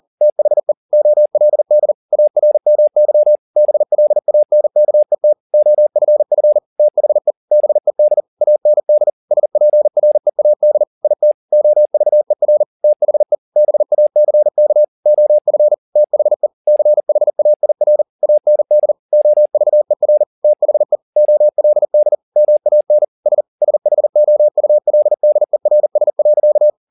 Morse code, heard as a keyed tone, about 40 wpm, 600 Hz. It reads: THE OLD ARMY BLANKET OFF THE BED AND SPREAD IT OVER THE BACK OF THE CHAIR AND OVER THE OLD MAN S SHOULDERS.